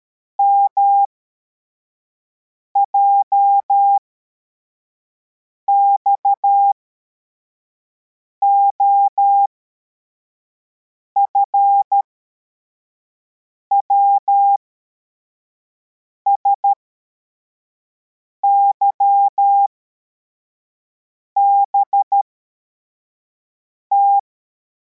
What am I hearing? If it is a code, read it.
Morse code, heard as a keyed tone, about 13 wpm, spaced as for 4 wpm, 800 Hz. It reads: MJXOF WSYBT
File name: formation / cw / Text84